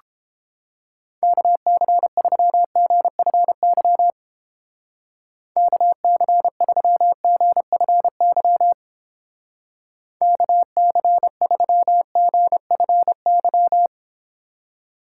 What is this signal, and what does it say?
Morse code, heard as a keyed tone, about 30 wpm, 700 Hz. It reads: KC3GFY KC3GFY KC3GFY